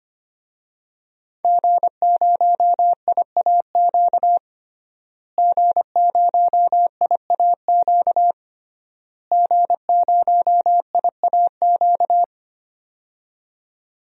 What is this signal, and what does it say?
Morse code, heard as a keyed tone, about 25 wpm, 700 Hz. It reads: G0IAQ G0IAQ G0IAQ
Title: DXCC-100 Call Signs - 5 Characters - Repeated 3x - 25wpm